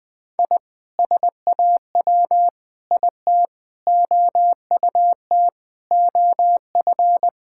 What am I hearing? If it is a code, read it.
Morse code, heard as a keyed tone, about 20 wpm, 700 Hz. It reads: I SAW IT OUT OF
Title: Sentences with 7-Word Phrases - 20wpm